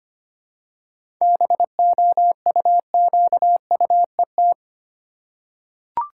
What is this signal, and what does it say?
Morse code, heard as a keyed tone, about 25 wpm, 700 Hz. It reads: BOUQUET
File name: practice / 25wpm / Easy Words with Hints - 25wpm